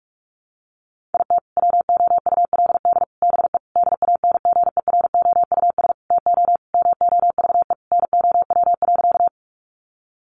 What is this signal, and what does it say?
Morse code, heard as a keyed tone, about 44 wpm, 700 Hz. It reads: IT WOULD BE DANGEROUS TO MOVE NOW.
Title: Sentences from Top 1200 Words 45wpm